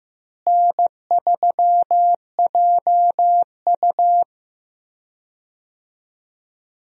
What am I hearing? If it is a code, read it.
Morse code, heard as a keyed tone, about 15 wpm, 700 Hz. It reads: N3JU
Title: USA Call Signs - 4 Characters - Repeated 3x - 15wpm